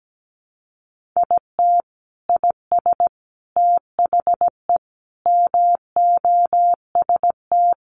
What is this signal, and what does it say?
Morse code, heard as a keyed tone, about 17 wpm, 700 Hz. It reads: IT IS THE MOST
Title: Sentences from Top 1500 Words 17wpm